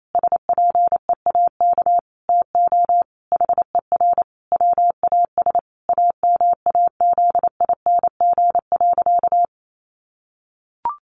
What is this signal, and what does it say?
Morse code, heard as a keyed tone, about 28 wpm, 700 Hz. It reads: SPEAK TO HER WAS AMAZING.
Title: Sentences from Top 1500 Words 28wpm